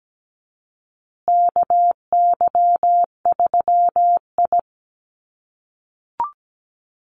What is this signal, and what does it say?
Morse code, heard as a keyed tone, about 17 wpm, 700 Hz. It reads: KY3I
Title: USA Call Signs Repeated 2x 17wpm